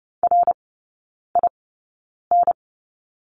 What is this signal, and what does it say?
Morse code, heard as a keyed tone, about 30 wpm, spaced as for 9 wpm, 700 Hz. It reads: RIN
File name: Lesson 18e - Letter m - Three Characters 30wpm